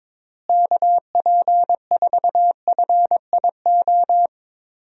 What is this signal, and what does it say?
Morse code, heard as a keyed tone, about 22 wpm, 700 Hz. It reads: KP4FIO
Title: USA Call Signs - 6 Characters - Repeated 3x - 22wpm